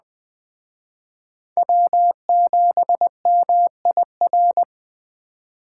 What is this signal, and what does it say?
Morse code, heard as a keyed tone, about 20 wpm, 700 Hz. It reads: W7MIR
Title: Speed Racing - USA 1x3 Call Signs 17wpm